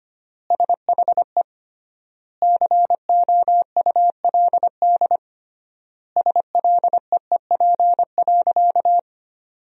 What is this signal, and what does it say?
Morse code, heard as a keyed tone, about 25 wpm, 700 Hz. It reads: SHE COULD SLEEP.
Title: 3x Word-Spacing - Sentences from Top 500 Words - 25wpm